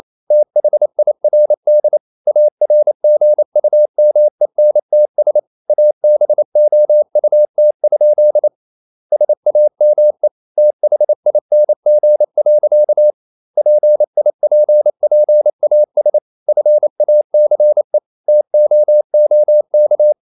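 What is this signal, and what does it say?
Morse code, heard as a keyed tone, about 28 wpm, 600 Hz. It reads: THIRD ARGUMENTS ABOUT? SAME THING. PIPPAS FACE TOOK
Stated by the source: Morse code practice at 28 wpm, text of Ken Follett's novel Never in English, from CW med Gnister.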